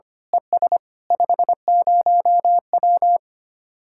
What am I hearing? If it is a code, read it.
Morse code, heard as a keyed tone, about 26 wpm, 700 Hz. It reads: ES 50W
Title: Speed-Racing Common QSO Phrases Encore - 25wpm